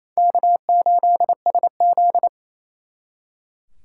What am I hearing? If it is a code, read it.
Morse code, heard as a keyed tone, about 35 wpm, 700 Hz. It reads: K8SZ